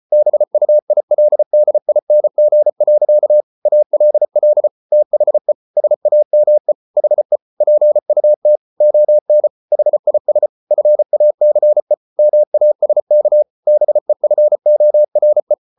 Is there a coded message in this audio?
Yes: Morse code from 0.1 to 15.5 s, heard as a keyed tone, about 34 wpm, 600 Hz. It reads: BUILDING. ALL THE SAME HE PUT ON HIS FACE MASK BEFORE